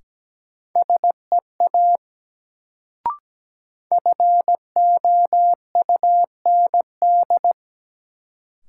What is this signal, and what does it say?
Morse code, heard as a keyed tone, about 17 wpm, 700 Hz. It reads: SEA FOUND
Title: Sentences from Top 300 Words - New Word Drill - 17wpm